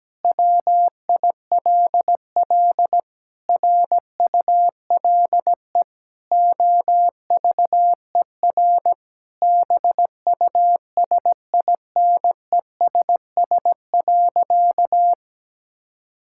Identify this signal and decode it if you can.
Morse code, heard as a keyed tone, about 17 wpm, 700 Hz. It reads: WILL RULE OVER BUSINESS.
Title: Sentences from Top 1000 Words - Review Part 5 - 17wpm